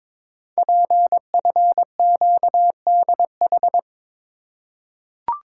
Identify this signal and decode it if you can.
Morse code, heard as a keyed tone, about 22 wpm, 700 Hz. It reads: PFQDH
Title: Code Groups - 5 Letters - 22wpm